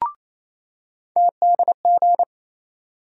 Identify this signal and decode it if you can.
Morse code, heard as a keyed tone, about 28 wpm, 700 Hz. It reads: TDG
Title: USA Call Sign Postfixes - 3 Letters 28wpm